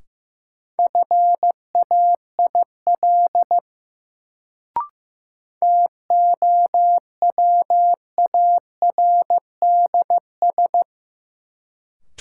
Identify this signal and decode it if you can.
Morse code, heard as a keyed tone, about 15 wpm, 700 Hz. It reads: FAIL TOWARDS
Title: Sentences from Top 1100 Words - New Word Drill Alt - 15wpm